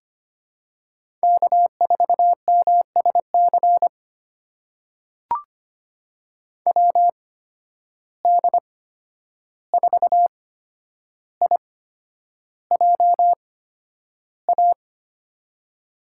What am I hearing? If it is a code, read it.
Morse code, heard as a keyed tone, about 25 wpm, spaced as for 10 wpm, 700 Hz. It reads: K4MSC WD4IJA
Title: USA Call Signs - 8x Character Spacing - 25wpm